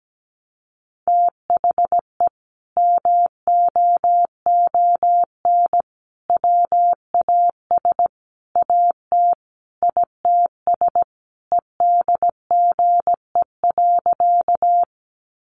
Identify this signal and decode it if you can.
Morse code, heard as a keyed tone, about 17 wpm, 700 Hz. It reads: THE MOON WAS AT ITS EDGE.
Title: Sentences from Top 1300 Words 17wpm